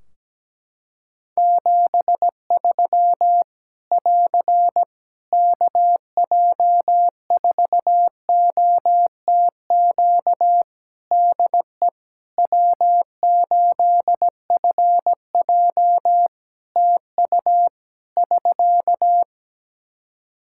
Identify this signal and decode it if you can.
Morse code, heard as a keyed tone, about 17 wpm, 700 Hz. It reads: 73 <AR> KJ4OTQ DE W8FJ TU <SK>
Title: Practice QSO - Exchange 4 - Ending - 17wpm